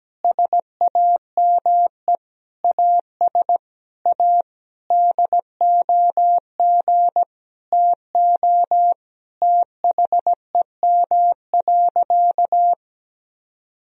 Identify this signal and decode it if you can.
Morse code, heard as a keyed tone, about 17 wpm, 700 Hz. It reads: SAME AS A DOG TO THEM.
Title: Sentences from Top 1000 Words - Review Part 2 - 17wpm